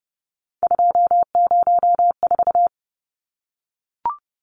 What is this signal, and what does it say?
Morse code, heard as a keyed tone, about 30 wpm, 700 Hz. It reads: 204
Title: Numbers - 3-Digit 30wpm